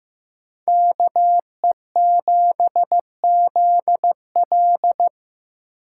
Morse code, heard as a keyed tone, about 17 wpm, 700 Hz. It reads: KE7ZL
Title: Speed Racing - USA 2x2 Call Signs 15wpm